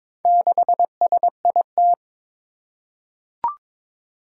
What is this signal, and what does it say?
Morse code, heard as a keyed tone, about 22 wpm, 700 Hz. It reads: 6SIT
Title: Four Challenging Letter-Number Combinations 22wpm